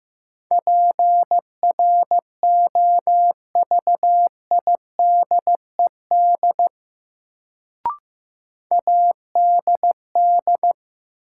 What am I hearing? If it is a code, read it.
Morse code, heard as a keyed tone, about 15 wpm, 700 Hz. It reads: PROVIDED ADD
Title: Sentences from Top 1500 Words - New Word Drill Alt - 15wpm